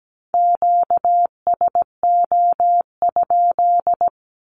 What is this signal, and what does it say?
Morse code, heard as a keyed tone, about 17 wpm, 700 Hz. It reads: QSO?
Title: Q-Codes with Spoken Meaning - 17wpm